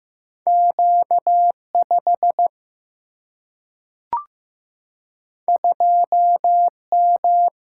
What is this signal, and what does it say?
Morse code, heard as a keyed tone, about 15 wpm, 700 Hz. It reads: Q 5 2 M
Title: Single-Letter Single-Number 15wpm